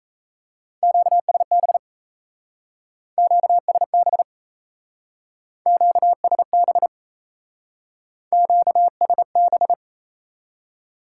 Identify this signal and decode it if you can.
Morse code, heard as a keyed tone, about 36 wpm, 700 Hz. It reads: QSB QSB QSB QSB